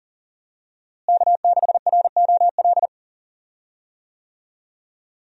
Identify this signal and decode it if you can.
Morse code, heard as a keyed tone, about 40 wpm, 700 Hz. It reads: K6ROL
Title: USA Call Signs - 5 Characters - Repeated 2x - 40wpm